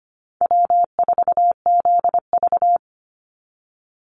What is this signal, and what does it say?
Morse code, heard as a keyed tone, about 25 wpm, 700 Hz. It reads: W4ZV
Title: NA SOTA Chaser Callsigns 25wpm